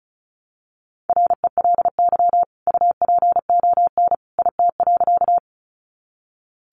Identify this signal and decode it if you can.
Morse code, heard as a keyed tone, about 34 wpm, 700 Hz. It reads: RELY UPON IT.
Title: Sentences from Top 1200 Words 35wpm